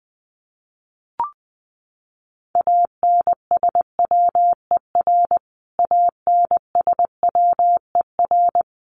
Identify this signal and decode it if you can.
Morse code, heard as a keyed tone, about 19 wpm, 700 Hz. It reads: ANSWER ANSWER
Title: Top 500 Words Repeated 20wpm